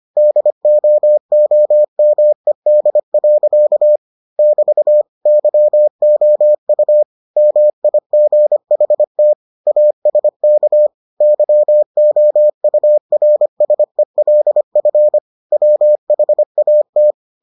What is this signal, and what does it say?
Morse code, heard as a keyed tone, about 25 wpm, 600 Hz. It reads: DOOMED. <BT> YOU MIGHT ASK YOURSELF WHAT